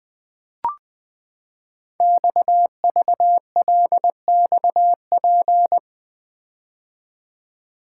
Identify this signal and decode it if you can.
Morse code, heard as a keyed tone, about 20 wpm, 700 Hz. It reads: XVLXP
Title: Code Groups - 5 Letters - 20wpm